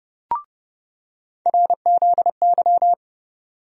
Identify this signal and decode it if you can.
Morse code, heard as a keyed tone, about 30 wpm, 700 Hz. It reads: RZY
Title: USA Call Sign Postfixes - 3 Letters 30wpm